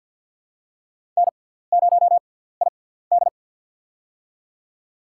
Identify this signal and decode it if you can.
Morse code, heard as a keyed tone, about 47 wpm, spaced as for 17 wpm, 700 Hz. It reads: N0ID
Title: USA Call Signs - 4 Characters - 6x Character Spacing - 50wpm